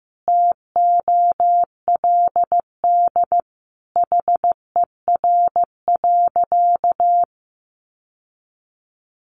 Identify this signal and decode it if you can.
Morse code, heard as a keyed tone, about 15 wpm, 700 Hz. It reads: TOLD HER.
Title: Sentences from Top 1200 Words 15wpm